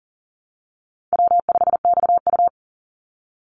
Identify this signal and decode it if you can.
Morse code, heard as a keyed tone, about 37 wpm, 700 Hz. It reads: W5XU
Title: Easy Call Signs Repeated 2x - 40wpm